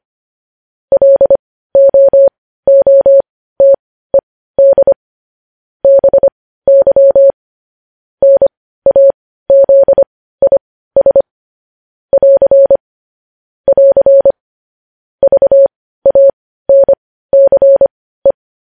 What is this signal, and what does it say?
Morse code, heard as a keyed tone, about 25 wpm, spaced as for 15 wpm, 550 Hz. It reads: LOOTED BY NAZIS <AR> <AR> VANCE
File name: podcast / News headlines-1x-25x15